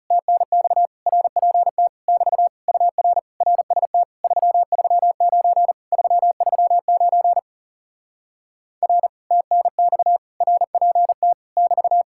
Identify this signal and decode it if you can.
Morse code, heard as a keyed tone, about 40 wpm, 700 Hz. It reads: TNX RPT <BT> UR RST 339 339 R TNX RPT <BT>